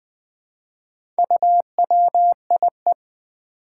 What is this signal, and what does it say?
Morse code, heard as a keyed tone, about 20 wpm, 700 Hz. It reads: UWIE
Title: Four Challenging Letter-Number Combinations 20wpm